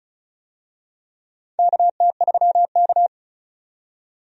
Morse code, heard as a keyed tone, about 36 wpm, 700 Hz. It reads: KT3K
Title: USA Call Signs - 4 Characters - 35wpm